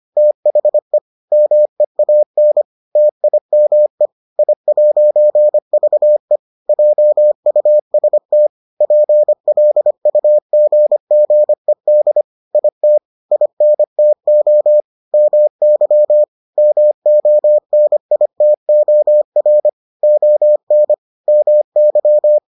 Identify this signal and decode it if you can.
Morse code, heard as a keyed tone, about 25 wpm, 600 Hz. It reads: THE MEAN TIME I'VE JUST PLUGGED IT INTO MY MONITOR ON MY